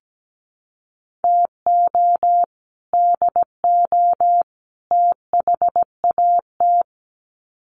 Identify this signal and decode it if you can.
Morse code, heard as a keyed tone, about 17 wpm, 700 Hz. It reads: TO DO THAT
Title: Sets of 3 Words 17wpm